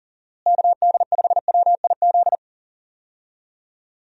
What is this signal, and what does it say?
Morse code, heard as a keyed tone, about 40 wpm, 700 Hz. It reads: KD5WIZ
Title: USA Call Signs - 6 Characters - Repeated 2x - 40wpm